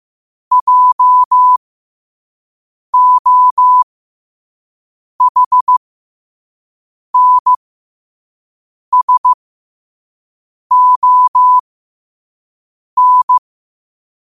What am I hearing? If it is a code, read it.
Morse code, heard as a keyed tone, about 15 wpm, spaced as for 5 wpm, 1000 Hz. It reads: JOHNSON
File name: qotdfiles / 5wpm